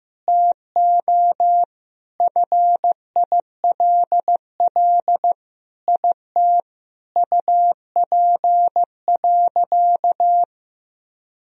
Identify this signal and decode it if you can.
Morse code, heard as a keyed tone, about 15 wpm, 700 Hz. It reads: TO FILL IT UP.
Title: Sentences from Top 1000 Words - Review Part 3 - 15wpm